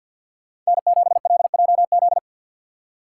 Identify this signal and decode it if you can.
Morse code, heard as a keyed tone, about 53 wpm, 700 Hz. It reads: N7LJZ
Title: USA Call Signs - 5 Characters - 6x Character Spacing - 50wpm